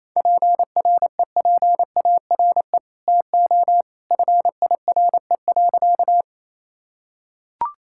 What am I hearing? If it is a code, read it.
Morse code, heard as a keyed tone, about 28 wpm, 700 Hz. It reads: PREPARE TO FIRE.
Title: Sentences from Top 1000 Words - Review Part 6 - 28wpm